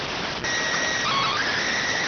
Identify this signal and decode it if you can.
SSTV (slow-scan television), PD 290.